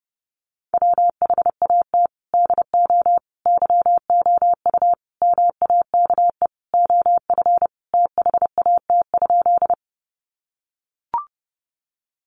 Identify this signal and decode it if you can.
Morse code, heard as a keyed tone, about 30 wpm, 700 Hz. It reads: WHAT DO YOU MAKE OF THAT?
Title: Sentences from Top 100 Words 30wpm